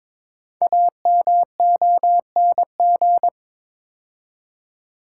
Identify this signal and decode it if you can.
Morse code, heard as a keyed tone, about 22 wpm, 690 Hz. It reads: AMONG
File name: 5 Letter Words - First 3 Letters Given - 22wpm